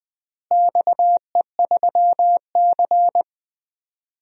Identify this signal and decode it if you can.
Morse code, heard as a keyed tone, about 20 wpm, 700 Hz. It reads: XE3C
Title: DXCC-100 Call Signs - 4 Characters - Repeated 3x - 20wpm